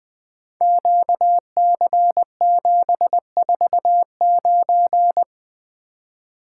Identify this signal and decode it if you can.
Morse code, heard as a keyed tone, about 20 wpm, 700 Hz. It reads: QC749
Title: Code Groups - 5 Letter:Numbers - 20wpm